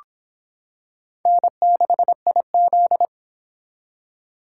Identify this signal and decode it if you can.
Morse code, heard as a keyed tone, about 25 wpm, 700 Hz. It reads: N6IZ